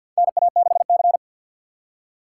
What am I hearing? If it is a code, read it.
Morse code, heard as a keyed tone, about 47 wpm, 700 Hz. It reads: NA6C
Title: USA Call Signs - 4 Characters - 3x Character Spacing - 50wpm